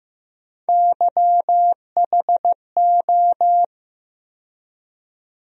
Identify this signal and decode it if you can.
Morse code, heard as a keyed tone, about 15 wpm, 700 Hz. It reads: YHO
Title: USA Call Sign Postfixes - 3 Letters 15wpm